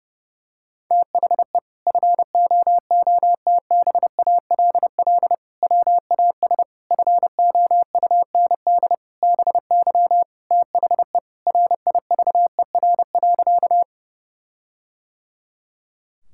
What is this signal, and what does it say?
Morse code, heard as a keyed tone, about 30 wpm, 700 Hz. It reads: THE FOOTBALL WAS FOUND BY THE RIVER.
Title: Sentences from Top 1600 Words 30wpm